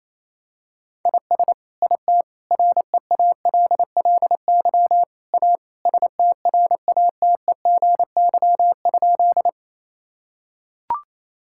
Morse code, heard as a keyed tone, about 27 wpm, 700 Hz. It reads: IS IT REALLY A STRATEGY?
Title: Sentences from Top 1000 Words - Review Part 5 - 28wpm